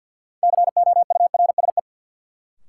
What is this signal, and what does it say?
Morse code, heard as a keyed tone, about 52 wpm, 700 Hz. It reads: COURSE
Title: Sets of 2 Words - New Word Drill - 50wpm